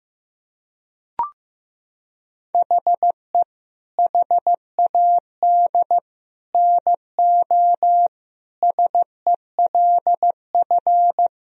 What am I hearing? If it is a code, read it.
Morse code, heard as a keyed tone, about 15 wpm, 700 Hz. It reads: HE HAD NO SELF
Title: Sentences from Top 1000 Words - Review Part 3 - 15wpm